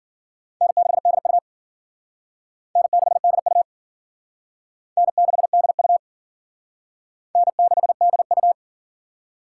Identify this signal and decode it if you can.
Morse code, heard as a keyed tone, about 50 wpm, 700 Hz. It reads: N6DU N6DU N6DU N6DU